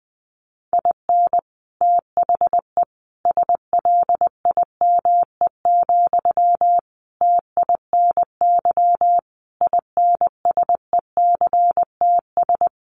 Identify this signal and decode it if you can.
Morse code, heard as a keyed tone, about 20 wpm, 700 Hz. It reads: IN THE SLIME, TINY INSECTS
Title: A Sound Of Thunder 20wpm